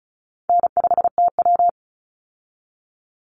Morse code, heard as a keyed tone, about 34 wpm, 700 Hz. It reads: N5TW
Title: Easy Call Signs Repeated 2x - 35wpm